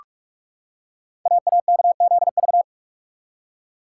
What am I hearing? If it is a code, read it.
Morse code, heard as a keyed tone, about 44 wpm, 700 Hz. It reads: AAKZV
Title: Code Groups - 5 Letter:Numbers - 45wpm